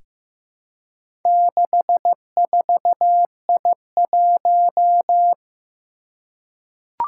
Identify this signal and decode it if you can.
Morse code, heard as a keyed tone, about 15 wpm, 700 Hz. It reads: 64I1
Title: Four Challenging Letter-Number Combinations 15wpm